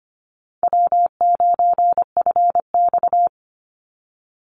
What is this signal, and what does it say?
Morse code, heard as a keyed tone, about 25 wpm, 700 Hz. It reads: W9FX
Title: Easy Call Signs Repeated 3x - 25wpm